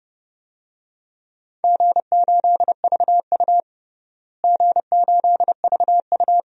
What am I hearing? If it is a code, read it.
Morse code, heard as a keyed tone, about 32 wpm, 700 Hz. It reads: G8VU G8VU
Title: DXCC-100 Call Signs - 4 Characters - Repeated 2x - 30wpm